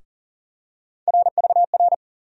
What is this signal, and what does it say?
Morse code, heard as a keyed tone, about 40 wpm, 700 Hz. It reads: RVR
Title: USA Call Sign Postfixes - 3 Letters 40wpm